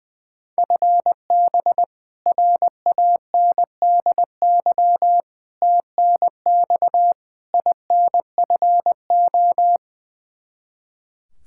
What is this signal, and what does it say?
Morse code, heard as a keyed tone, about 20 wpm, 700 Hz. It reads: FB RANDY TNX INFO
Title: Practice QSO - Exchange 3 - More Info - 20wpm